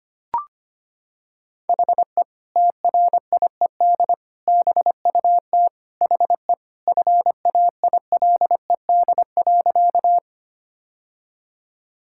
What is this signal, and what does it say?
Morse code, heard as a keyed tone, about 25 wpm, 700 Hz. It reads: HE TRIED BUT HE FAILED.
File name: practice / 25wpm / Sentences with 2-Word Phrases - 25wpm